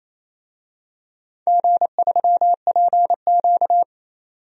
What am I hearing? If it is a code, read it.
Morse code, heard as a keyed tone, about 28 wpm, 700 Hz. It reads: G3PQ
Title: DXCC-100 Call Signs - 4 Characters - 28wpm